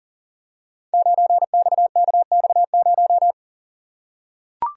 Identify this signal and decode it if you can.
Morse code, heard as a keyed tone, about 41 wpm, 700 Hz. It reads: 9XKX0
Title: Code Groups - 5 Letter:Numbers - 40wpm